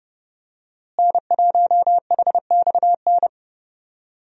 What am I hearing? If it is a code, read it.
Morse code, heard as a keyed tone, about 30 wpm, 700 Hz. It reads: N1HXN
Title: USA Call Signs - 5 Characters - Repeated 3x - 30wpm